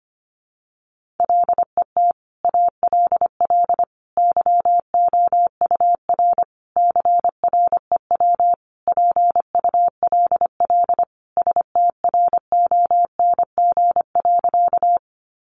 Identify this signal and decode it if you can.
Morse code, heard as a keyed tone, about 25 wpm, 700 Hz. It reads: LET ALL YOUR CREW PULL STRONG.
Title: Sentences from Top 1500 Words 25wpm